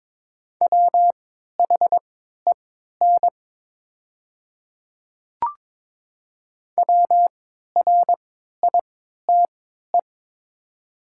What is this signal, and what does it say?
Morse code, heard as a keyed tone, about 22 wpm, spaced as for 12 wpm, 700 Hz. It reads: WHEN WRITE
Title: 3x Character Spacing - Top 100 Words - 22wpm